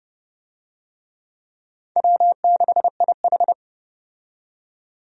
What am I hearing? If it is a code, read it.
Morse code, heard as a keyed tone, about 30 wpm, 700 Hz. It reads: W6IH
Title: USA Call Signs - 4 Characters - 30wpm